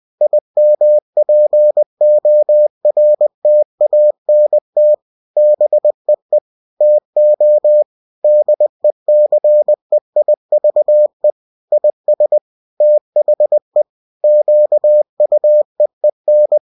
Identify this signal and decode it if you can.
Morse code, heard as a keyed tone, about 20 wpm, 600 Hz. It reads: IMPORTANT BEE TO DECEIVE IS THE QUEEN